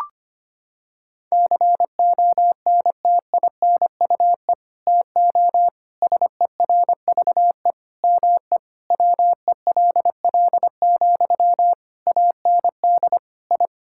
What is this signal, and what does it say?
Morse code, heard as a keyed tone, about 25 wpm, 700 Hz. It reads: CONTINUE TO SERVE ME WELL, AND I
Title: Sentences from Top 1000 Words - Review Part 6 - 25wpm